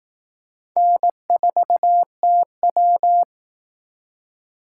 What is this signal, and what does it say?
Morse code, heard as a keyed tone, about 18 wpm, 700 Hz. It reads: N4TW
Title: Speed Racing - USA 1x2 Call Signs 15wpm